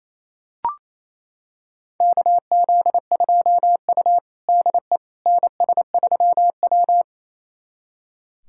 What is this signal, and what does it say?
Morse code, heard as a keyed tone, about 28 wpm, 700 Hz. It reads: KZ2U DE NS3W
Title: USA Callsign DE Callsign - 4 Characters - 28wpm